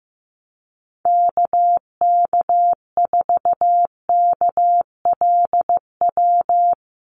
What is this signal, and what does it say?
Morse code, heard as a keyed tone, about 15 wpm, 700 Hz. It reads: KK4KLW
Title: USA Call Signs Repeated 3x 15wpm